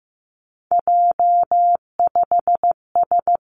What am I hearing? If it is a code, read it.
Morse code, heard as a keyed tone, about 15 wpm, 700 Hz. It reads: J5S
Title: Three Challenging Letter-Number Combinations 15wpm